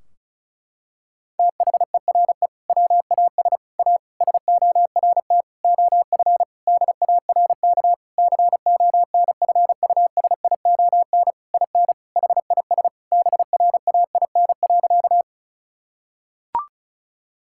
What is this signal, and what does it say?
Morse code, heard as a keyed tone, about 36 wpm, 700 Hz. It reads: THERE WAS A SORT OF DARK CONFUSION IN HIS BRAIN.
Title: Sentences from Top 1600 Words 35wpm